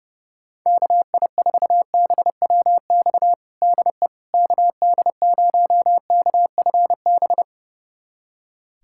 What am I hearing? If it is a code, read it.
Morse code, heard as a keyed tone, about 32 wpm, 700 Hz. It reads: KI4BWX DE KD0KFB
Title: USA Callsign DE Callsign - 6 Characters - 30wpm